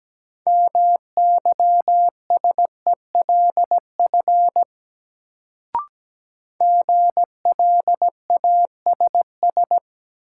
Morse code, heard as a keyed tone, about 17 wpm, 700 Hz. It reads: MYSELF GLASS
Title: Sentences from Top 1300 Words - New Word Drill Alt - 17wpm